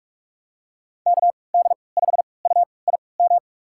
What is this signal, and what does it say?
Morse code, heard as a keyed tone, about 46 wpm, spaced as for 25 wpm, 700 Hz. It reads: KD5UIM
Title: USA Call Signs - 6 Characters - 3x Character Spacing - 45wpm